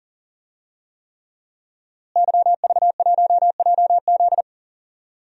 Morse code, heard as a keyed tone, about 40 wpm, 700 Hz. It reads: YV1JZ
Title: DXCC-100 Call Signs - 5 Characters - Repeated 3x - 40wpm